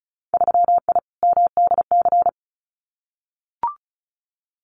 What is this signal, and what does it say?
Morse code, heard as a keyed tone, about 36 wpm, 700 Hz. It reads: 3I MDC
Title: Winter Field Day Exchanges Encore 35wpm